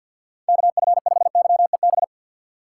Morse code, heard as a keyed tone, about 52 wpm, 700 Hz. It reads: KF5YEB
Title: USA Call Signs - 6x Character Spacing - 50wpm